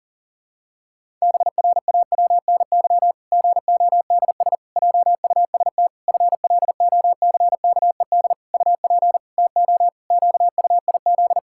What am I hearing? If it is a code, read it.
Morse code, heard as a keyed tone, about 40 wpm, 700 Hz. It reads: BRAWNY GODS JUST FLOCKED UP TO QUIZ
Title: Pangram Challenge v2 - 40wpm